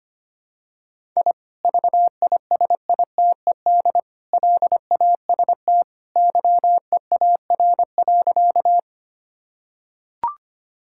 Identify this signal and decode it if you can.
Morse code, heard as a keyed tone, about 25 wpm, 700 Hz. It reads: I VISITED LAST YEAR.
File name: practice / 25wpm / Sentences with 2-Word Phrases - 25wpm